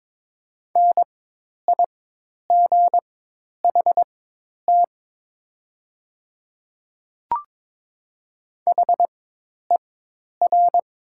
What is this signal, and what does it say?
Morse code, heard as a keyed tone, about 22 wpm, spaced as for 10 wpm, 700 Hz. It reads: NIGHT HER
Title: 4x Character Spacing - Top 300 Words - 22wpm